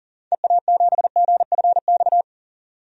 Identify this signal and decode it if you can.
Morse code, heard as a keyed tone, about 40 wpm, 700 Hz. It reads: EA7GFX
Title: DXCC-100 Call Signs - 6 Characters - 40wpm